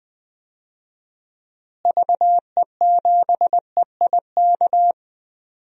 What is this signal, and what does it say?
Morse code, heard as a keyed tone, about 20 wpm, 700 Hz. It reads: VE7EIK
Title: DXCC-100 Call Signs - 6 Characters - Repeated 3x - 20wpm